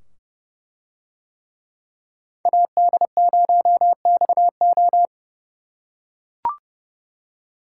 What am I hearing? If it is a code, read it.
Morse code, heard as a keyed tone, about 29 wpm, 700 Hz. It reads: AD0XO